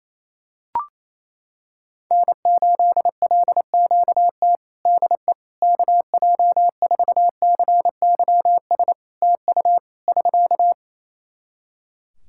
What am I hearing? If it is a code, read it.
Morse code, heard as a keyed tone, about 28 wpm, 700 Hz. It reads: N8LQT DE KJ4CYS TU <SK>
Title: K1USN SST QSO Practice 28wpm